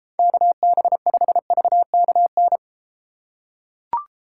Morse code, heard as a keyed tone, about 29 wpm, 690 Hz. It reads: KB5VKN